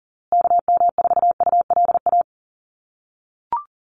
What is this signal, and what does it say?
Morse code, heard as a keyed tone, about 40 wpm, 700 Hz. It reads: KM4ULA
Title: USA Call Signs Repeated 2x 40wpm